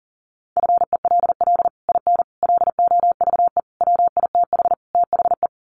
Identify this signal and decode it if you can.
Morse code, heard as a keyed tone, about 40 wpm, 700 Hz. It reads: FELL IN LOVE WITH THE
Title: Sets of 5 Words Encore2 40wpm